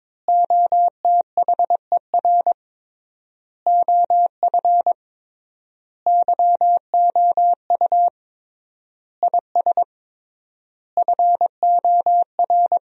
Morse code, heard as a keyed tone, about 22 wpm, 700 Hz. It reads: OTHER OF YOU IS FOR
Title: 3x Word-Spacing - Sentences from Top 400 Words - 22wpm